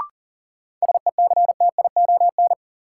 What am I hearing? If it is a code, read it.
Morse code, heard as a keyed tone, about 40 wpm, 700 Hz. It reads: SECTION
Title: Sentences from Top 1000 Words - Review Part 5 - New Word Drill - 40wpm